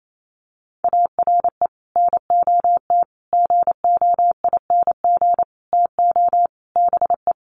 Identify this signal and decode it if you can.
Morse code, heard as a keyed tone, about 28 wpm, 700 Hz. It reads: ARE NOT GOING TO BE
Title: Sets of 5 Words Encore 28wpm